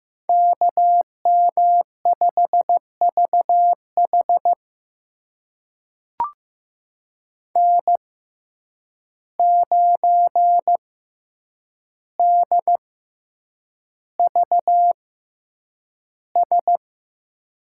Morse code, heard as a keyed tone, about 15 wpm, spaced as for 8 wpm, 700 Hz. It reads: KM5VH N9DVS
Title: USA Call Signs - 5 Characters - 6x Character Spacing - 15wpm